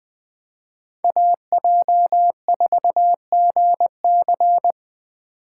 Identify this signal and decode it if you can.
Morse code, heard as a keyed tone, about 20 wpm, 700 Hz. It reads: AJ4GC